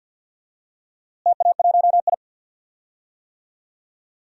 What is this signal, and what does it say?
Morse code, heard as a keyed tone, about 51 wpm, 700 Hz. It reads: TA1I
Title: DXCC-100 Call Signs - 4 Characters - Repeated 3x - 50wpm